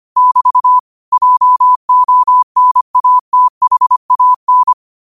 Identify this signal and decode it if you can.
Morse code, heard as a keyed tone, about 25 wpm, 1000 Hz. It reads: <BT> JONATHAN
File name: qotdfiles / 25wpm